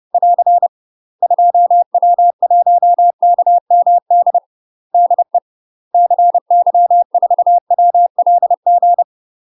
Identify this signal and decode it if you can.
Morse code, heard as a keyed tone, about 30 wpm, 700 Hz. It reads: <AR> 2W1KMD DE CY4WLG
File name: UK MASTER QSOs TRAINING / 30WPM / UK MASTER QSOs 0013